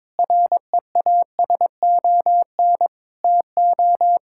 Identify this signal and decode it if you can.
Morse code, heard as a keyed tone, about 22 wpm, 700 Hz. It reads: REASON TO
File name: Sentences with 7-Word Phrases - 22wpm